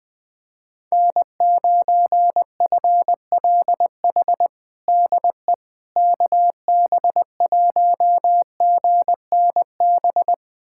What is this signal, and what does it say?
Morse code, heard as a keyed tone, about 20 wpm, 700 Hz. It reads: N9FLH DE KB1GNB
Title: USA Callsign DE Callsign 20wpm